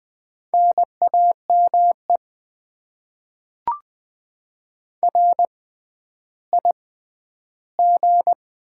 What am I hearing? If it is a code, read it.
Morse code, heard as a keyed tone, about 20 wpm, spaced as for 11 wpm, 700 Hz. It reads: NAME RIG
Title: Top 20 QSO Elements - 6x Character Spacing - 20wpm